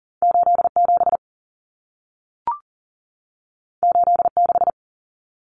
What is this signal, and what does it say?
Morse code, heard as a keyed tone, about 40 wpm, 700 Hz. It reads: 87 86
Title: Common Cut Numbers - 2-Digit 40wpm